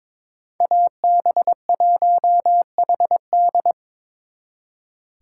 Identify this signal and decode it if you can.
Morse code, heard as a keyed tone, about 26 wpm, 700 Hz. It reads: AB1HD
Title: Speed Racing - USA 2x2 Call Signs 22wpm